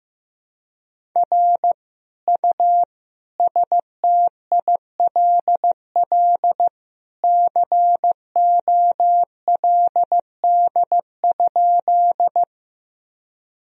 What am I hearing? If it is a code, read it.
Morse code, heard as a keyed tone, about 16 wpm, 700 Hz. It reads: R U STILL COLD?
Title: Sentences from Top 400 Words and Abbreviations 15wpm